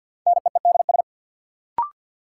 Morse code, heard as a keyed tone, about 51 wpm, 700 Hz. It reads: NEEDS
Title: Sentences from Top 1000 Words - Review Part 3 - New Word Drill - 50wpm